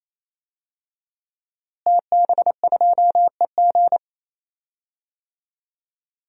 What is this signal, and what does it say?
Morse code, heard as a keyed tone, about 27 wpm, 700 Hz. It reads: TB2EG
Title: DXCC-100 Call Signs - 5 Characters - Repeated 3x - 28wpm